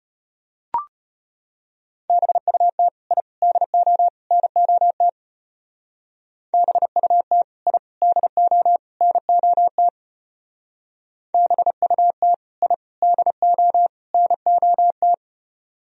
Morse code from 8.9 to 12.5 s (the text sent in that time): NOT BUT